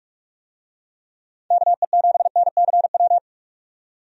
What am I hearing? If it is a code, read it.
Morse code, heard as a keyed tone, about 44 wpm, 700 Hz. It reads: KE7NCW
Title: USA Call Signs - 6 Characters - 45wpm